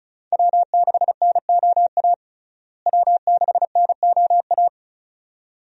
Morse code, heard as a keyed tone, about 36 wpm, 700 Hz. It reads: W6NOA W6NOA
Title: USA Call Signs - 5 Characters - Repeated 2x - 35wpm